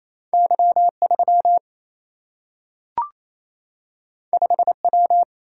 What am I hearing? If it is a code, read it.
Morse code, heard as a keyed tone, about 28 wpm, 700 Hz. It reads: Y3 5W
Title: Single-Letter Single-Number 28wpm